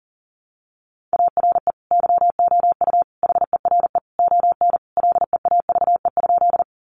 Morse code, heard as a keyed tone, about 40 wpm, 700 Hz. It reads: ARE YOU HERE ON LEAVE?
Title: Sentences from Top 400 Words 40wpm